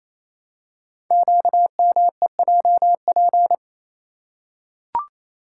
Morse code, heard as a keyed tone, about 28 wpm, 700 Hz. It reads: QMEJP
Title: Code Groups - 5 Letter:Numbers - 28wpm